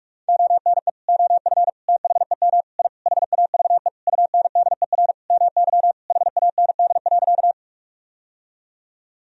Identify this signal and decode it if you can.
Morse code, heard as a keyed tone, about 46 wpm, 700 Hz. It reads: ONE OF THEM I HAVE UNDER MY HAND.
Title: Sentences from Top 1000 Words - Review Part 1 - 45wpm